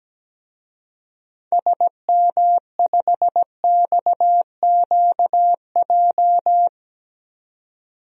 Morse code, heard as a keyed tone, about 17 wpm, 700 Hz. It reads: SM5XQJ
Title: DXCC-100 Call Signs - 6 Characters - 17wpm